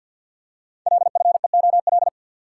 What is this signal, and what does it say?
Morse code, heard as a keyed tone, about 48 wpm, 700 Hz. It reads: LFEOL
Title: Code Groups - 5 Letters - 50wpm